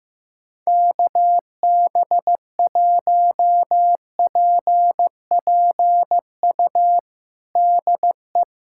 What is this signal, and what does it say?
Morse code, heard as a keyed tone, about 15 wpm, 700 Hz. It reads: KB1PPU DE
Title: USA Callsign DE Callsign - 6 Characters - 15wpm